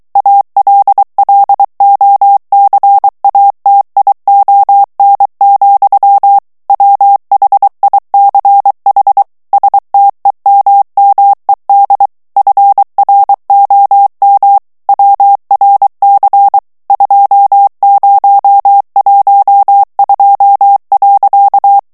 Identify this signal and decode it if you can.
Morse code, heard as a keyed tone, about 23 wpm, 800 Hz. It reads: ALLOCATION, WHICH STEMMED FROM WRC 2012.
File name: ANGLAIS texte clair 3 vitesse 22 mots minute